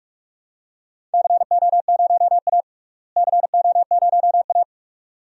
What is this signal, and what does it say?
Morse code, heard as a keyed tone, about 42 wpm, 700 Hz. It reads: CO0A CO0A
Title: DXCC-100 Call Signs - 4 Characters - Repeated 2x - 45wpm